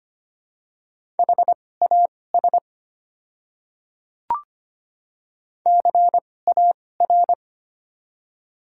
Morse code, heard as a keyed tone, about 25 wpm, spaced as for 18 wpm, 700 Hz. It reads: HAS CAR
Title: 2x Character Spacing - Top 300 Words - 25wpm